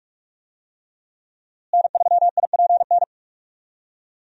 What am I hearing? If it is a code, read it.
Morse code, heard as a keyed tone, about 46 wpm, 700 Hz. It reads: N3IPN
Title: USA Call Signs - 5 Characters - Repeated 3x - 45wpm